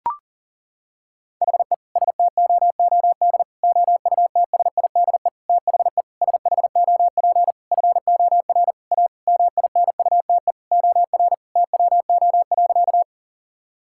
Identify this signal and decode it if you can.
Morse code, heard as a keyed tone, about 37 wpm, 700 Hz. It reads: HE STOOD OUTSIDE THE SHOP FOR A MINUTE OR TWO.
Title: Sentences from Top 1600 Words 40wpm